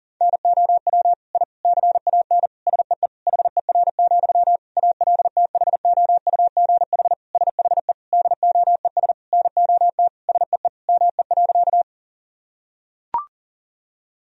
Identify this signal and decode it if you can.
Morse code, heard as a keyed tone, about 40 wpm, 700 Hz. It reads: NOW I CAN SEE HER, ALTHOUGH SHE DOES NOT SEE ME.
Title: Sentences from Top 1000 Words - Review Part 3 - 40wpm